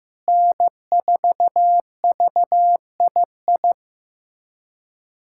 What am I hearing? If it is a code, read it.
Morse code, heard as a keyed tone, about 15 wpm, 700 Hz. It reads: N4VII
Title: USA Callsign DE Callsign 15wpm